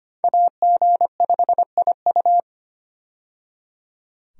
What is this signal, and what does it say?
Morse code, heard as a keyed tone, about 25 wpm, 700 Hz. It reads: AG5IU
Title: USA Call Signs - 5 Characters - 25wpm